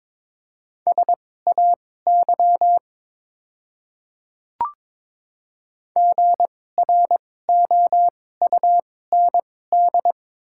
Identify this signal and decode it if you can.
Morse code, heard as a keyed tone, about 22 wpm, spaced as for 16 wpm, 690 Hz. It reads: SAY GROUND
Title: 2x Character Spacing - Top 500 Words - 22wpm